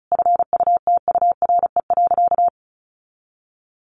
Morse code, heard as a keyed tone, about 34 wpm, 700 Hz. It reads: FUTURE.
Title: Sentences from Top 1200 Words 35wpm